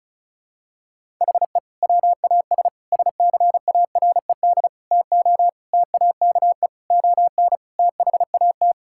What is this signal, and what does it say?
Morse code, heard as a keyed tone, about 36 wpm, 700 Hz. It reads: HE WAS SCARED TO TAKE ON THAT
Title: Sentences from Top 1000 Words - Review Part 6 - 35wpm